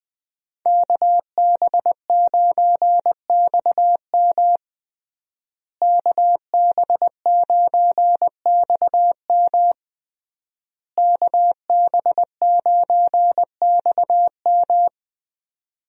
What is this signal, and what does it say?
Morse code, heard as a keyed tone, about 21 wpm, 700 Hz. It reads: KB9XM KB9XM KB9XM
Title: USA Call Signs - 5 Characters - Repeated 3x - 20wpm